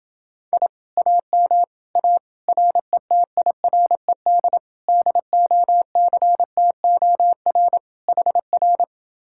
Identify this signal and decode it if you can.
Morse code, heard as a keyed tone, about 30 wpm, 700 Hz. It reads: I AM A RETIRED DOCTOR HR